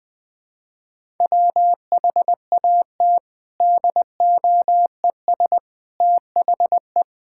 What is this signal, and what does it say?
Morse code, heard as a keyed tone, about 21 wpm, 700 Hz. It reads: WHAT DOES THE
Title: Sentences from Top 1000 Words - Review Part 6 - 20wpm